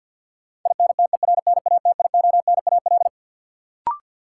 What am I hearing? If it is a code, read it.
Morse code, heard as a keyed tone, about 52 wpm, 700 Hz. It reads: INTERNATIONAL
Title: Sentences from Top 1000 Words - Review Part 4 - New Word Drill - 50wpm